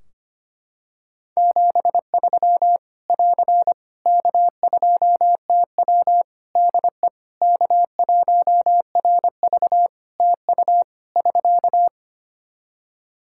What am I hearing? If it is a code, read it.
Morse code, heard as a keyed tone, about 25 wpm, 700 Hz. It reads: 73 <AR> K2TW DE K1RV TU <SK>
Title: K1USN SST QSO Practice 25wpm